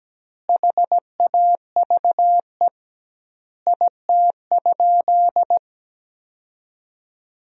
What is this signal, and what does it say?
Morse code, heard as a keyed tone, about 17 wpm, 700 Hz. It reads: HAVE IT?
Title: 2x Word-Spacing - Sentences from Top 100 Words - 17wpm